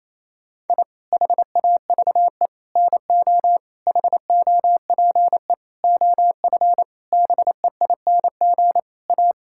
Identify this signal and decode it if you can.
Morse code, heard as a keyed tone, about 28 wpm, 700 Hz. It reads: I HAVE NO HOPE OF BEING A
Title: Sentences from Top 1000 Words - Review Part 5 - 28wpm